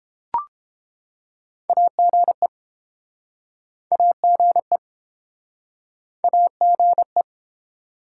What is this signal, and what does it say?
Morse code, heard as a keyed tone, about 32 wpm, 700 Hz. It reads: AGE AGE AGE